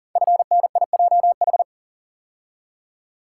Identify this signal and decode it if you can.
Morse code, heard as a keyed tone, about 40 wpm, 700 Hz. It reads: FNIJH
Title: Code Groups - 5 Letters - 40wpm